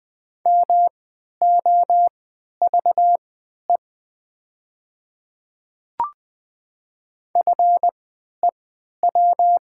Morse code, heard as a keyed tone, about 20 wpm, spaced as for 11 wpm, 700 Hz. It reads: MOVE FEW
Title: 3x Character Spacing - Top 300 Words - 20wpm